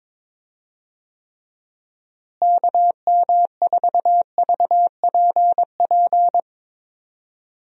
Morse code, heard as a keyed tone, about 22 wpm, 700 Hz. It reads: KM4VPP
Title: USA Call Signs - 6 Characters - Repeated 3x - 22wpm